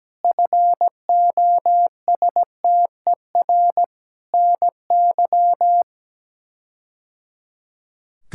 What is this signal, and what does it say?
Morse code, heard as a keyed tone, about 17 wpm, 700 Hz. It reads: FOSTER NY
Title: K1USN SST QSO Practice - 17wpm